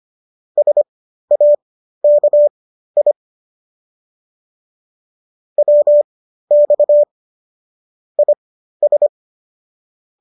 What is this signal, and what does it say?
Morse code, heard as a keyed tone, about 25 wpm, spaced as for 13 wpm, 590 Hz. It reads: SAKI WX IS